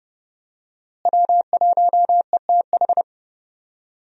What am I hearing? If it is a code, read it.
Morse code, heard as a keyed tone, about 30 wpm, 700 Hz. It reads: W1ETH
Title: USA Call Signs - 5 Characters - Repeated 2x - 30wpm